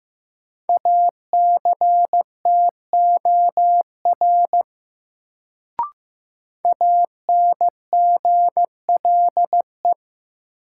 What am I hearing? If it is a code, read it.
Morse code, heard as a keyed tone, about 15 wpm, 700 Hz. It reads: ACTOR ANGLE
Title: Sentences from Top 1500 Words - New Word Drill Alt - 15wpm